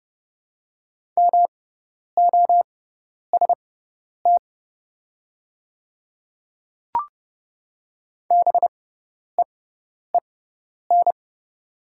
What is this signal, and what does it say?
Morse code, heard as a keyed tone, about 30 wpm, spaced as for 10 wpm, 700 Hz. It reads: MOST BEEN
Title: 6x Character Spacing - Top 100 Words - 30wpm